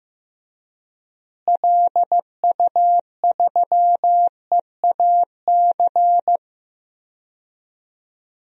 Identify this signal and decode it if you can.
Morse code, heard as a keyed tone, about 15 wpm, 700 Hz. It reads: LU3EAC
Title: DXCC-100 Call Signs - 6 Characters - Repeated 2x - 15wpm